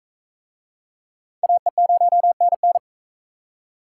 Morse code, heard as a keyed tone, about 43 wpm, 700 Hz. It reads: AE0NN